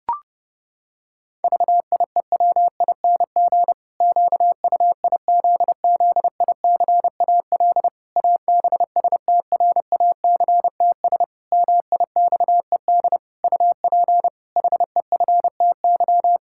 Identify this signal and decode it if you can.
Morse code, heard as a keyed tone, about 32 wpm, 700 Hz. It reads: VIEWING QUIZZICAL ABSTRACTS MIXED UP HEFTY
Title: Pangram Challenge v2 - 30wpm